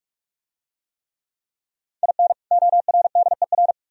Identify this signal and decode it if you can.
Morse code, heard as a keyed tone, about 44 wpm, 700 Hz. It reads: IN ORDER